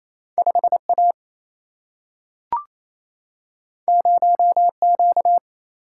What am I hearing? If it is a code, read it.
Morse code, heard as a keyed tone, about 28 wpm, 700 Hz. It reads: 5A 0Q
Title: Single-Letter Single-Number 28wpm